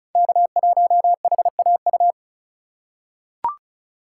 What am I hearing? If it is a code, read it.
Morse code, heard as a keyed tone, about 36 wpm, 700 Hz. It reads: K1HAU
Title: USA Call Signs - 5 Characters - 3x Character Spacing - 35wpm